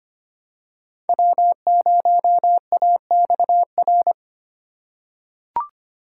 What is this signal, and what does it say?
Morse code, heard as a keyed tone, about 25 wpm, 700 Hz. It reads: W0AXR
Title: USA Call Signs - 3x Character Spacing - 25wpm